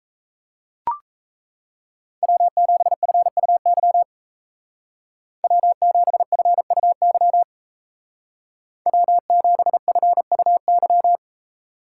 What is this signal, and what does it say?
Morse code, heard as a keyed tone, about 37 wpm, 700 Hz. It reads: W7FUY W7FUY W7FUY